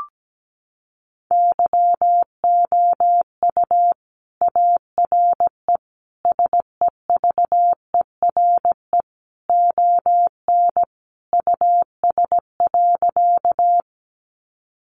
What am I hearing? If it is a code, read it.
Morse code, heard as a keyed tone, about 17 wpm, 700 Hz. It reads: YOU ARE SEVERE ON US.
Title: Sentences from Top 1400 Words 17wpm